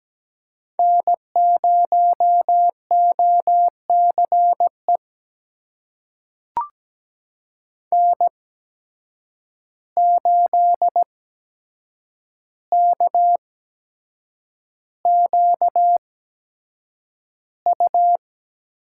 Morse code, heard as a keyed tone, about 17 wpm, spaced as for 7 wpm, 700 Hz. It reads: N0OCE N8KQU
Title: USA Call Signs - 8x Character Spacing - 17wpm